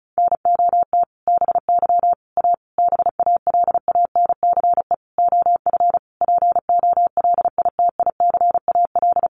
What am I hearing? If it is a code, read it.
Morse code, heard as a keyed tone, about 34 wpm, 700 Hz. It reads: NOT BY A BALANCE OF POLITICAL
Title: Sentences from Top 900 Words 35wpm